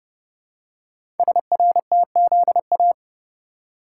Morse code, heard as a keyed tone, about 30 wpm, 700 Hz. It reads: SRTZA
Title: Code Groups - 5 Letters - 30wpm